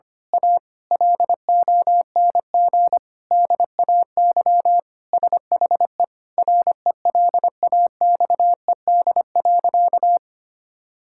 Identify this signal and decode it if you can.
Morse code, heard as a keyed tone, about 25 wpm, 700 Hz. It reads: A LONG DAY SHE RELAXED.
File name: practice / 25wpm / Sentences with 4-Word Phrases - 25wpm